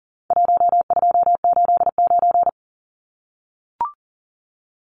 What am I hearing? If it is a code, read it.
Morse code, heard as a keyed tone, about 39 wpm, 700 Hz. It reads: 1289
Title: Numbers - Contest Style - 4-Digit 40wpm